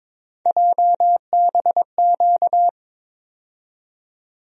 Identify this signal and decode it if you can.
Morse code, heard as a keyed tone, about 22 wpm, 700 Hz. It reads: JBQ
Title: USA Call Sign Postfixes - 3 Letters 22wpm